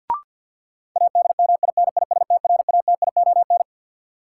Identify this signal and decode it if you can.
Morse code, heard as a keyed tone, about 53 wpm, 700 Hz. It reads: ADMINISTRATION
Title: Sentences from Top 1300 Words - New Word Drill - 50wpm